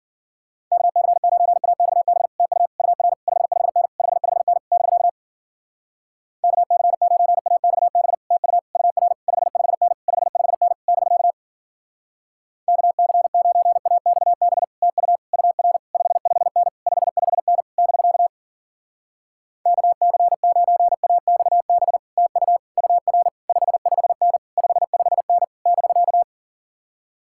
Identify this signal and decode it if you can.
Morse code, heard as a keyed tone, about 53 wpm, 700 Hz. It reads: KC9AXB TU UR 55N 55N <BK> KC9AXB TU UR 55N 55N <BK> KC9AXB TU UR 55N 55N <BK> KC9AXB TU UR 55N 55N <BK>